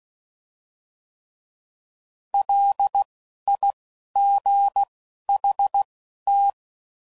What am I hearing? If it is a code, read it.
Morse code, heard as a keyed tone, about 16 wpm, spaced as for 11 wpm, 800 Hz. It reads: LIGHT